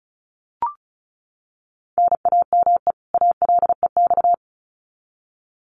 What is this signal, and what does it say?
Morse code, heard as a keyed tone, about 34 wpm, 700 Hz. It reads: NAME ALEX
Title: Common QSO Phrases 35wpm